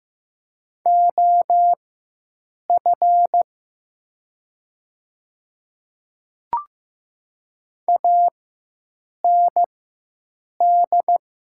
Morse code, heard as a keyed tone, about 15 wpm, spaced as for 7 wpm, 700 Hz. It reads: OF AND
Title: 4x Character Spacing - Top 100 Words - 15wpm